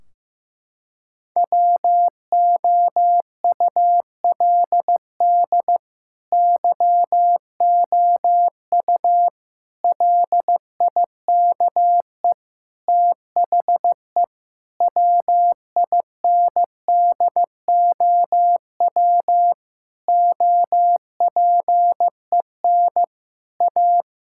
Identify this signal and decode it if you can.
Morse code, heard as a keyed tone, about 15 wpm, 700 Hz. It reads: WOULD YOU LIKE THE WINDOW OPEN A
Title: Sentences from Top 1600 Words 15wpm